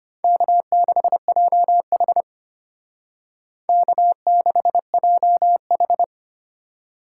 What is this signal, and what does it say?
Morse code, heard as a keyed tone, about 30 wpm, 700 Hz. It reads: K6JH K6JH
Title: Speed Racing - USA 1x2 Call Signs 25wpm